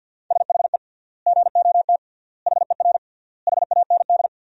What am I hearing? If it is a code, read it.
Morse code, heard as a keyed tone, about 48 wpm, 700 Hz. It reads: SHE GOT HER HAND
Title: 3x Word-Spacing - Sentences from Top 500 Words - 50wpm